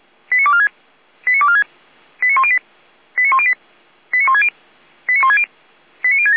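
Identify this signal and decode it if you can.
Five-tone selective call (ZVEI), heard as quick runs of five tones, tones of about 70 ms. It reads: 89246 89246 89108 89108 89160 89160 89067
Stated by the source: ZVEI five-tone selective call received by SDR, from feuerwehr-orlamuende.org (Orlamünde fire brigade).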